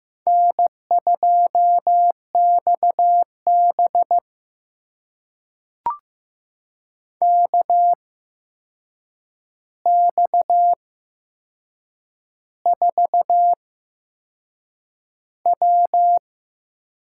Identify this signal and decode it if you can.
Morse code, heard as a keyed tone, about 15 wpm, spaced as for 6 wpm, 700 Hz. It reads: N2XB KX4W
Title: USA Call Signs - 4 Characters - 8x Character Spacing - 15wpm